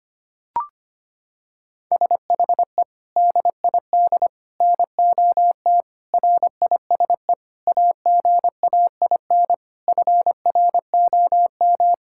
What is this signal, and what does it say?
Morse code, heard as a keyed tone, about 25 wpm, 700 Hz. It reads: SHE DID NOT RISE AGAIN FROM
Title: Sentences from Top 1000 Words - Review Part 6 - 25wpm